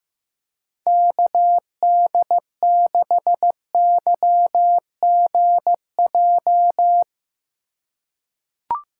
Morse code, heard as a keyed tone, about 15 wpm, 700 Hz. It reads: KD6YGJ
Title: USA Call Signs - 3x Character Spacing - 15wpm